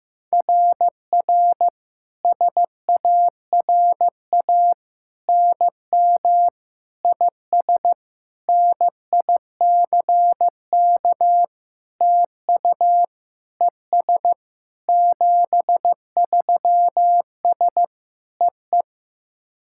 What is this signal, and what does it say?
Morse code, heard as a keyed tone, about 15 wpm, 700 Hz. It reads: RR SARA NM IS NICK TU ES 73S EE
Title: SOTA QSO Practice 15wpm